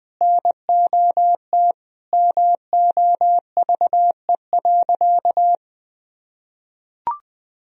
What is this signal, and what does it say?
Morse code, heard as a keyed tone, about 20 wpm, 700 Hz. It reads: NOT MOVE.
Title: Sentences from Top 1000 Words - Review Part 1 - 20wpm